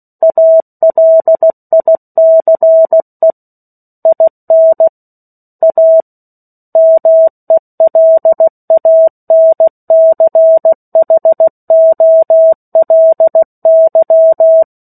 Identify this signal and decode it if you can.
Morse code, heard as a keyed tone, about 16 wpm, 650 Hz. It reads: ALICE IN A MELANCHOLY
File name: 03-Chapter-03 16 WPM